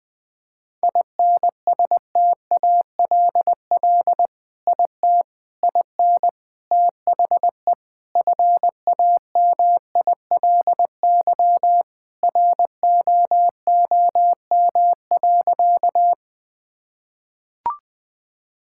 Morse code, heard as a keyed tone, about 20 wpm, 700 Hz. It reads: INSTALL IT IN THE FAMILY ROOM.
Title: Sentences from Top 1600 Words 20wpm